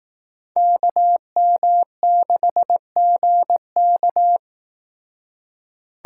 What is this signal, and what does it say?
Morse code, heard as a keyed tone, about 18 wpm, 700 Hz. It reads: KM6GK
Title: Speed Racing - USA 2x2 Call Signs 15wpm